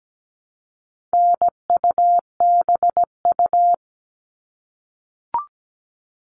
Morse code, heard as a keyed tone, about 17 wpm, 700 Hz. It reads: NUBU
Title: All Cut Numbers - 4-Digit 17wpm